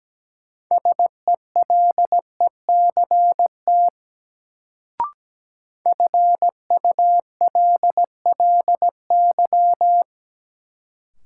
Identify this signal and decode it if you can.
Morse code, heard as a keyed tone, about 17 wpm, 700 Hz. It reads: SELECT FULLY
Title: Sentences from Top 1300 Words - New Word Drill - 17wpm